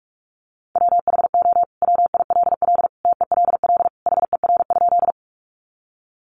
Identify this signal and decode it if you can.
Morse code, heard as a keyed tone, about 44 wpm, 700 Hz. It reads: WHO WILL TELL HER?
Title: Sentences from Top 200 Words 45wpm